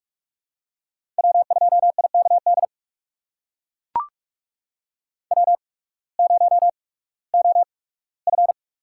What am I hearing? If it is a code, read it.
Morse code, heard as a keyed tone, about 42 wpm, spaced as for 20 wpm, 700 Hz. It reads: W2IKD W0OF
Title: USA Call Signs - 5 Characters - 8x Character Spacing - 45wpm